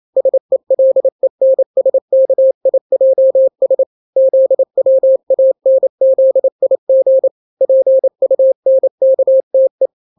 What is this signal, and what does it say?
Morse code, heard as a keyed tone, about 28 wpm, 530 Hz. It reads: SELENSKIJS ZWANZIG PUNKTE
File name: daily / 27wpm